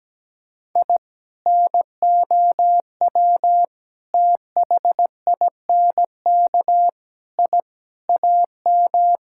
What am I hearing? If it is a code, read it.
Morse code, heard as a keyed tone, about 17 wpm, 700 Hz. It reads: I NOW THINK I AM
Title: Sentences from Top 1000 Words - Review Part 5 - 17wpm